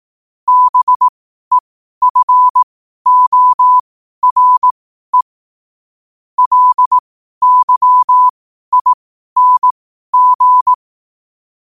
Morse code, heard as a keyed tone, about 18 wpm, spaced as for 12 wpm, 1000 Hz. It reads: BEFORE LYING
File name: qotdfiles / 12wpm